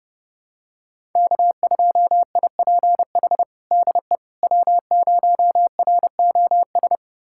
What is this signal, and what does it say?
Morse code, heard as a keyed tone, about 30 wpm, 700 Hz. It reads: K2IPH DE W0ROS
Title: USA Callsign DE Callsign - 5 Characters - 30wpm